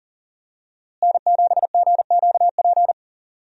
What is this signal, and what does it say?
Morse code, heard as a keyed tone, about 40 wpm, 700 Hz. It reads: N7GQP
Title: USA Call Signs - 5 Characters - Repeated 3x - 40wpm